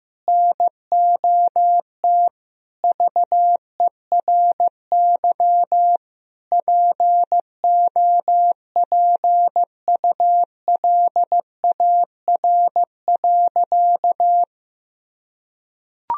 Morse code, heard as a keyed tone, about 16 wpm, 700 Hz. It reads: NOT VERY POPULAR.
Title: Sentences from Top 1000 Words - Review Part 3 - 15wpm